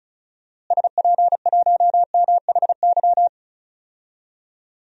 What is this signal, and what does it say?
Morse code, heard as a keyed tone, about 36 wpm, 700 Hz. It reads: SP1MHY
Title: DXCC-100 Call Signs - 6 Characters - Repeated 3x - 35wpm